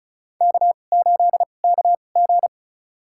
Morse code, heard as a keyed tone, about 34 wpm, spaced as for 25 wpm, 700 Hz. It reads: K8KG
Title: USA Call Signs - 4 Characters - 2x Character Spacing - 35wpm